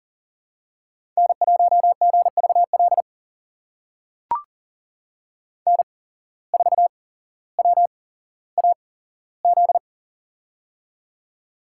Morse code, heard as a keyed tone, about 37 wpm, spaced as for 17 wpm, 700 Hz. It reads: N1GVL N4WAZ
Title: USA Call Signs - 5 Characters - 8x Character Spacing - 40wpm